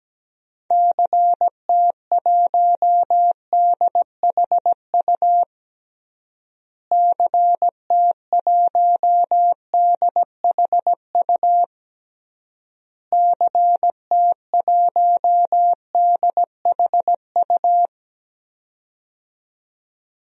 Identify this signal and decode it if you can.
Morse code, heard as a keyed tone, about 17 wpm, 700 Hz. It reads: CT1DHU CT1DHU CT1DHU
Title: DXCC-100 Call Signs - 6 Characters - Repeated 3x - 17wpm